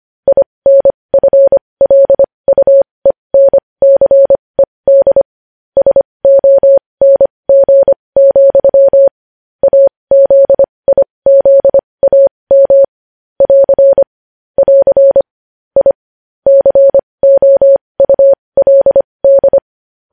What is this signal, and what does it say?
Morse code, heard as a keyed tone, about 25 wpm, spaced as for 20 wpm, 550 Hz. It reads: INFLUENCED SONG, AZIZAM <AR> <AR> I COULD
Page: News headlines in morse code at 20 wpm